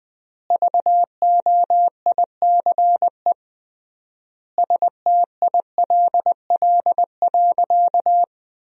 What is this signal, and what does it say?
Morse code, heard as a keyed tone, about 20 wpm, 700 Hz. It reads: VOICE STILL.
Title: 3x Word-Spacing - Sentences from Top 500 Words - 20wpm